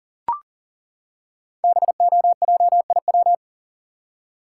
Morse code, heard as a keyed tone, about 43 wpm, 700 Hz. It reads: BOJIW
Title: Code Groups - 5 Letters - 40wpm